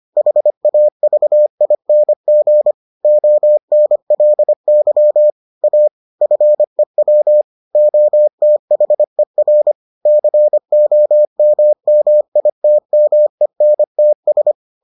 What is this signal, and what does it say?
Morse code, heard as a keyed tone, about 25 wpm, 600 Hz. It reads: HAVING ONLY A FEW OTHER COMMITMENTS